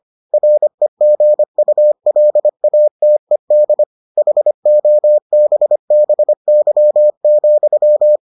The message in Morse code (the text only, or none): REGULATED HOBBY,